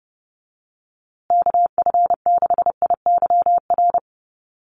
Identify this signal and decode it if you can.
Morse code, heard as a keyed tone, about 30 wpm, 700 Hz. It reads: KF6IYR
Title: USA Call Signs Repeated 3x 30wpm